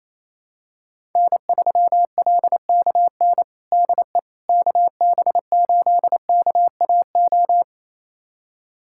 Morse code, heard as a keyed tone, about 28 wpm, 700 Hz. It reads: N3LKN DE KB8KAO
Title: USA Callsign DE Callsign 28wpm